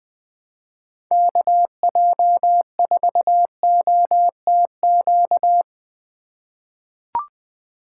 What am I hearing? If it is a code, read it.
Morse code, heard as a keyed tone, about 20 wpm, 700 Hz. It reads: KJ4OTQ
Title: USA Call Signs - 6 Characters - 4x Character Spacing - 20wpm